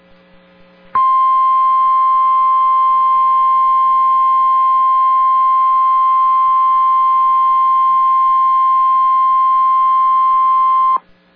Emergency Alert System (SAME) tones (the 1050 Hz attention tone). It is heard from 0.9 to 11.0 s.